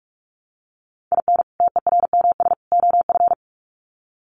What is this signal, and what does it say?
Morse code, heard as a keyed tone, about 44 wpm, 700 Hz. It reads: IN TERMS OF
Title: Sets of 3 Words 45wpm